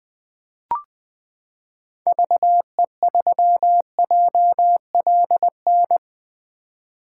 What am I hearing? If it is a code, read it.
Morse code, heard as a keyed tone, about 20 wpm, 700 Hz. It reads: VE3JLN
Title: 2023 POTA Activator Call Signs - 20wpm